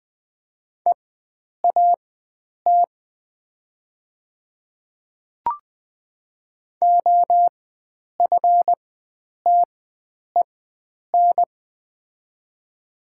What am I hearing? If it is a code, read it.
Morse code, heard as a keyed tone, about 20 wpm, spaced as for 9 wpm, 700 Hz. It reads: EAT OFTEN
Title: 4x Character Spacing - Top 500 Words - 20wpm